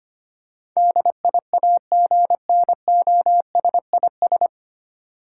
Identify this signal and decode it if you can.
Morse code, heard as a keyed tone, about 25 wpm, 700 Hz. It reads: DIAGNOSIS
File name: Nine Letter Words - 25wpm